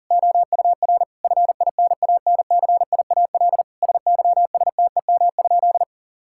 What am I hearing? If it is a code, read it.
Morse code, heard as a keyed tone, about 40 wpm, 700 Hz. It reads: OUR FINANCIAL SYSTEM?
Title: Sentences from Top 1000 Words - Review Part 5 - 40wpm